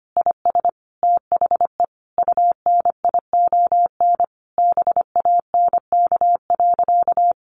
Morse code, heard as a keyed tone, about 25 wpm, 700 Hz. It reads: IS THE UNION BANK.
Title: Sentences from Top 1400 Words 25wpm